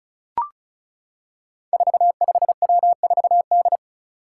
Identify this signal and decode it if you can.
Morse code, heard as a keyed tone, about 36 wpm, 700 Hz. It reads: 45W4D
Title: Five Challenging Letter-Number Combinations 35wpm